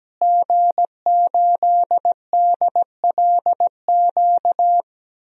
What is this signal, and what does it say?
Morse code, heard as a keyed tone, about 17 wpm, 700 Hz. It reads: G8DLQ
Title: DXCC-100 Call Signs - 5 Characters - Repeated 2x - 17wpm